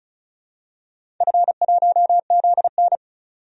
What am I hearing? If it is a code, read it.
Morse code, heard as a keyed tone, about 34 wpm, 700 Hz. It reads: F1ZN
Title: DXCC-100 Call Signs - 4 Characters - 35wpm